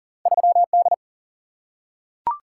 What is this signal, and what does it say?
Morse code, heard as a keyed tone, about 40 wpm, 700 Hz. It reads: 3D
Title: Single-Letter Single-Number 40wpm